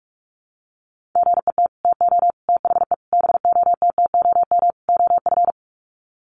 Morse code, heard as a keyed tone, about 44 wpm, 700 Hz. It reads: GET TO THE BOTTOM OF
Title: Sets of 5 Words Encore 45wpm